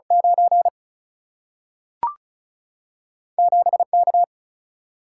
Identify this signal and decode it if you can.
Morse code, heard as a keyed tone, about 36 wpm, 700 Hz. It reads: 9 7K
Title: Single-Letter Single-Number 35wpm